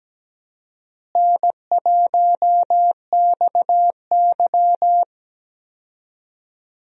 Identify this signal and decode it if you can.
Morse code, heard as a keyed tone, about 17 wpm, 700 Hz. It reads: N1XY
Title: USA Call Signs - 4 Characters - Repeated 3x - 17wpm